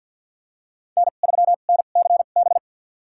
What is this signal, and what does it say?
Morse code, heard as a keyed tone, about 51 wpm, spaced as for 36 wpm, 700 Hz. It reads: N3NCB
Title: USA Call Signs - 5 Characters - 2x Character Spacing - 50wpm